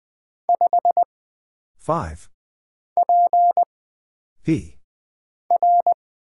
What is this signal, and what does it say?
Morse code, heard as a keyed tone, about 20 wpm, 700 Hz. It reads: 5 P R
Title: Single Character (Letter:Number:Punctuation:Prosign) - Mind-Melt - 20wpm